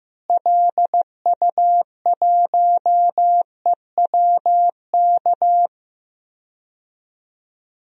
Morse code, heard as a keyed tone, about 15 wpm, 700 Hz. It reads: LU1EWK
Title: DXCC-100 Call Signs - 6 Characters - Repeated 3x - 15wpm